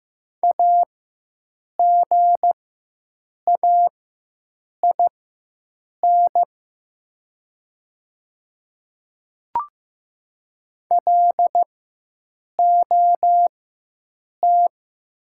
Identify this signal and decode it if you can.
Morse code, heard as a keyed tone, about 15 wpm, spaced as for 7 wpm, 700 Hz. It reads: AGAIN LOT
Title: 4x Character Spacing - Top 500 Words - 15wpm